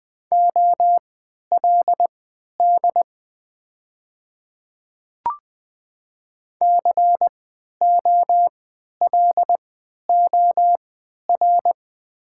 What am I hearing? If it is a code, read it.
Morse code, heard as a keyed tone, about 20 wpm, spaced as for 11 wpm, 690 Hz. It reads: OLD COLOR
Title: 3x Character Spacing - Top 300 Words - 20wpm